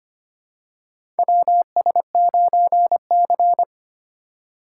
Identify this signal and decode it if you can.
Morse code, heard as a keyed tone, about 25 wpm, 700 Hz. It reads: WS9C
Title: USA Call Signs - 4 Characters - 2x Character Spacing - 25wpm